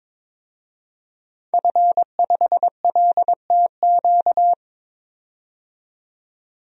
Morse code, heard as a keyed tone, about 22 wpm, 700 Hz. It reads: F5LTQ
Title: DXCC-100 Call Signs - 5 Characters - 22wpm